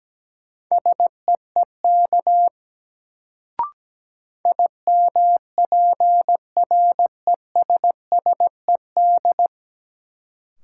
Morse code, heard as a keyed tone, about 17 wpm, 700 Hz. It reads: SEEK IMPRESSED
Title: Sentences from Top 1600 Words - New Word Drill Alt - 17wpm